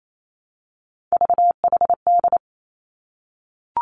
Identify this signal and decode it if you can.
Morse code, heard as a keyed tone, about 28 wpm, 700 Hz. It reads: VHD
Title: Three Challenging Letter-Number Combinations 28wpm